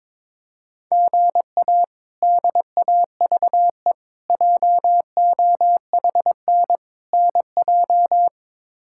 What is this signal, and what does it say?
Morse code, heard as a keyed tone, about 21 wpm, 700 Hz. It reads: GA DAVE JOHN NJ
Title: K1USN SST QSO Practice - 22wpm